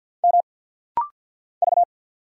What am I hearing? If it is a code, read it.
Morse code, heard as a keyed tone, about 50 wpm, 700 Hz. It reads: M V
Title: Single Letter-Number Alt - Rapid-Fire 50wpm